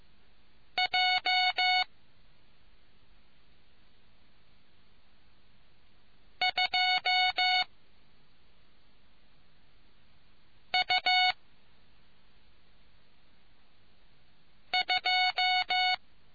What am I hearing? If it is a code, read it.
Morse code, heard as a keyed tone, about 15 wpm, 740 Hz. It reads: J 2 U 2